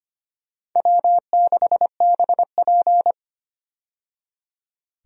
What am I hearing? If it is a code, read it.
Morse code, heard as a keyed tone, about 25 wpm, 700 Hz. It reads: W6BP
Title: USA Call Signs - 4 Characters - 25wpm